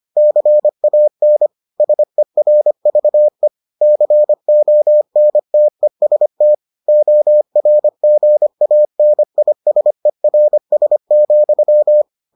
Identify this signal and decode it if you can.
Morse code, heard as a keyed tone, about 25 wpm, 600 Hz. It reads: CAN SERVE CONTEST ORGANISERS,